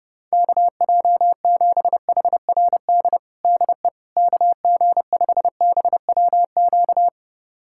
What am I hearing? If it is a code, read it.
Morse code, heard as a keyed tone, about 29 wpm, 700 Hz. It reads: KJ7HRD DE KG5BWQ
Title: USA Callsign DE Callsign 30wpm